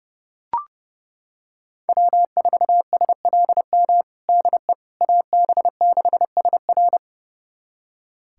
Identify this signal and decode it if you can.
Morse code, heard as a keyed tone, about 32 wpm, 700 Hz. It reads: W4SLM DE AB6SR
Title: USA Callsign DE Callsign - 5 Characters - 30wpm